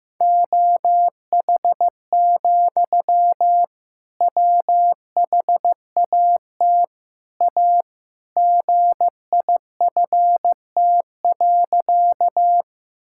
Morse code, heard as a keyed tone, about 15 wpm, 700 Hz. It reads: OH, WHAT A GIFT.
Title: Sentences from Top 1000 Words - Review Part 6 - 15wpm